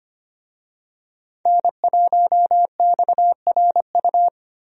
Morse code, heard as a keyed tone, about 25 wpm, 700 Hz. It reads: N1XRU
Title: USA Call Signs - 5 Characters - Repeated 2x - 25wpm